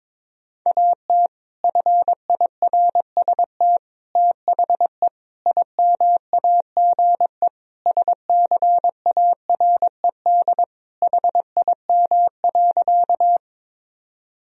Morse code, heard as a keyed tone, about 22 wpm, 700 Hz. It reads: AT FIRST THE IMAGE SCARED HIM.
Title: Sentences from Top 1000 Words - Review Part 4 - 22wpm